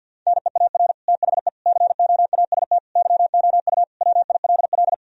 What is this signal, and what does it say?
Morse code, heard as a keyed tone, about 50 wpm, 700 Hz. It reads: NEAR THE COAST YOU WILL
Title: Sentences from Top 1000 Words - Review Part 4 - 50wpm